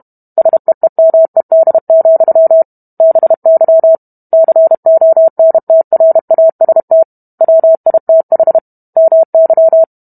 Morse code, heard as a keyed tone, about 32 wpm, 650 Hz. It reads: SEEMED, BY CONTRAST WITH MY